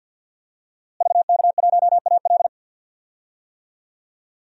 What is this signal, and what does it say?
Morse code, heard as a keyed tone, about 50 wpm, 700 Hz. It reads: VK1AL
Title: DXCC-100 Call Signs - 5 Characters - Repeated 3x - 50wpm